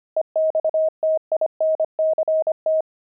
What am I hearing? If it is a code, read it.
Morse code, heard as a keyed tone, about 25 wpm, 630 Hz. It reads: EXTINCT
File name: pangram-fs-25-cs-25-kidding